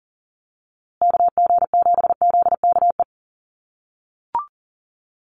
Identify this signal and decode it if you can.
Morse code, heard as a keyed tone, about 40 wpm, 700 Hz. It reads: KG7ZKE
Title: USA Call Signs Repeated 3x 40wpm